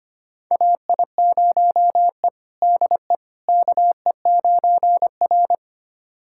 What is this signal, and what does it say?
Morse code, heard as a keyed tone, about 25 wpm, 700 Hz. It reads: AI0E DE KE9R
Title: USA Callsign DE Callsign - 4 Characters - 25wpm